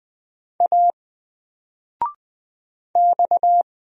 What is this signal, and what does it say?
Morse code, heard as a keyed tone, about 20 wpm, 700 Hz. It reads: A X
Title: Single Letter-Number 20wpm